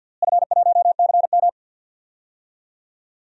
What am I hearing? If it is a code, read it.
Morse code, heard as a keyed tone, about 50 wpm, 700 Hz. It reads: F1CM
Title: DXCC-100 Call Signs - 4 Characters - Repeated 2x - 50wpm